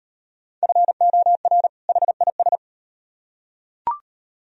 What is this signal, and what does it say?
Morse code, heard as a keyed tone, about 38 wpm, 700 Hz. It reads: FOR HIS